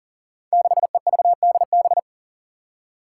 Morse code, heard as a keyed tone, about 38 wpm, 700 Hz. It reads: 6EVDB
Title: Five Challenging Letter-Number Combinations 40wpm